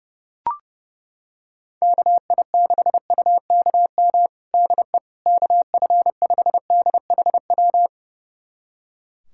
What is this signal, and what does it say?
Morse code, heard as a keyed tone, about 29 wpm, 700 Hz. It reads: KI6UKM DE KF5DHW
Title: USA Callsign DE Callsign - 6 Characters - 30wpm